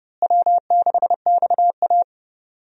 Morse code, heard as a keyed tone, about 32 wpm, 700 Hz. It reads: W6XA
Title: USA Call Signs - 4 Characters - Repeated 3x - 30wpm